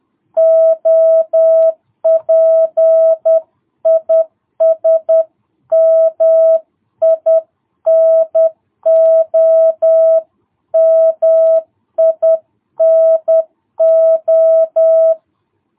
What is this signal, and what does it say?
Morse code, heard as a keyed tone, about 10 wpm, 660 Hz. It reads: OP IS MINO MINO
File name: jh8cbh / 2022cw